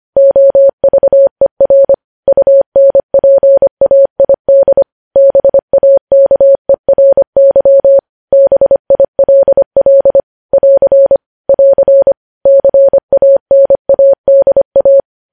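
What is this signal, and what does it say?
Morse code, heard as a keyed tone, about 25 wpm, 550 Hz. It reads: OVER UNPAID BAKERY BILL <AR> <AR> CANADA
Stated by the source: News headlines sent in Morse code at 25 wpm.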